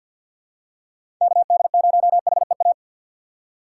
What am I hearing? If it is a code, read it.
Morse code, heard as a keyed tone, about 53 wpm, 700 Hz. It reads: KD0HEA
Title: USA Call Signs - 6 Characters - 4x Character Spacing - 50wpm